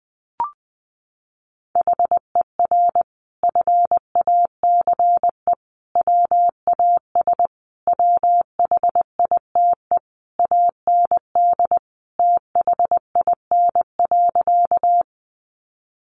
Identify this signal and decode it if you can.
Morse code, heard as a keyed tone, about 20 wpm, 700 Hz. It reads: HER FACE WAS WHITE AND THIN.
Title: Sentences from Top 1200 Words 20wpm